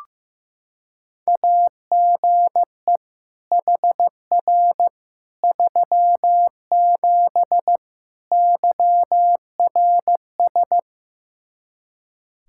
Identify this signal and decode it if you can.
Morse code, heard as a keyed tone, about 15 wpm, 700 Hz. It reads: AGE HR 37 YRS
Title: Practice QSO - Exchange 3 - More Info - 15wpm